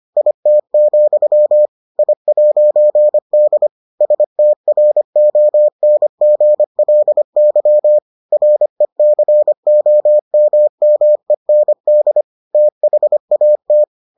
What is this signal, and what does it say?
Morse code, heard as a keyed tone, about 25 wpm, 600 Hz. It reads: IT, I'D STRONGLY RECOMMEND THAT